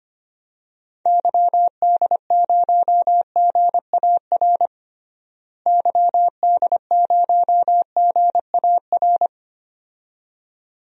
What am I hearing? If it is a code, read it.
Morse code, heard as a keyed tone, about 25 wpm, 700 Hz. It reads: YD0GAR YD0GAR
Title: DXCC-100 Call Signs - 6 Characters - Repeated 2x - 25wpm